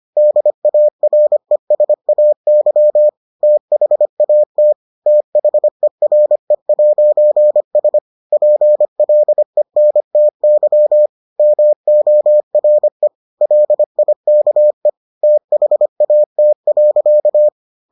Morse code, heard as a keyed tone, about 25 wpm, 600 Hz. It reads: DARESAY THAT THERE'S PLENTY MORE LIKE THAT.